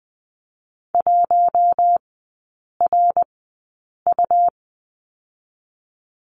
Morse code, heard as a keyed tone, about 20 wpm, spaced as for 8 wpm, 700 Hz. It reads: 1RU
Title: 100 Lesson 17e - Letter u - Three Characters 20wpm